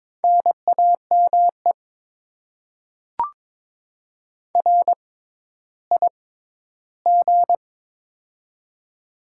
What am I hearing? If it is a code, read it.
Morse code, heard as a keyed tone, about 23 wpm, spaced as for 13 wpm, 700 Hz. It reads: NAME RIG
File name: Top 20 QSO Elements - 6x Character Spacing - 22wpm